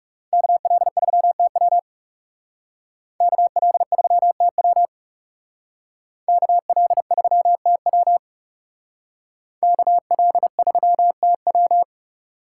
Morse code, heard as a keyed tone, about 36 wpm, 700 Hz. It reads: KL3TW KL3TW KL3TW KL3TW
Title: Speed Racing - USA 2x2 Call Signs 30wpm